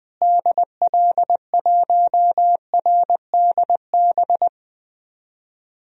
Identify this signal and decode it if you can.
Morse code, heard as a keyed tone, about 20 wpm, 700 Hz. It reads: DL1RDB
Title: DXCC-100 Call Signs - 6 Characters - Repeated 3x - 20wpm